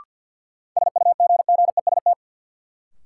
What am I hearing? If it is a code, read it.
Morse code, heard as a keyed tone, about 51 wpm, 700 Hz. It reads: SUGGEST
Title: Sentences from Top 1300 Words - New Word Drill Alt - 50wpm